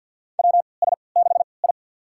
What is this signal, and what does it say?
Morse code, heard as a keyed tone, about 46 wpm, spaced as for 27 wpm, 700 Hz. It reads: WS6I
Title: USA Call Signs - 4 Characters - 3x Character Spacing - 50wpm